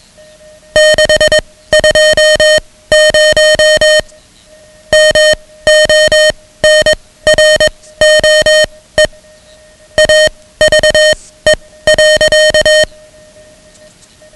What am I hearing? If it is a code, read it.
Morse code, heard as a keyed tone, about 21 wpm, spaced as for 16 wpm, 620 Hz, with noise. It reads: 620 MONROE AVE.
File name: ham radio / CodeQuick / Code Quick 10